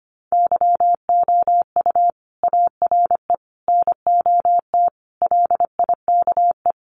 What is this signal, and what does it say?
Morse code, heard as a keyed tone, about 25 wpm, 700 Hz. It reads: YOU ARE NOT LIKE
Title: Sentences from Top 600 Words 25wpm